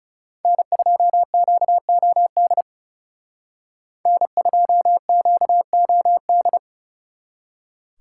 Morse code, heard as a keyed tone, about 36 wpm, 700 Hz. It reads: N2QOD N2QOD
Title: Speed Racing - USA 1x3 Call Signs 30wpm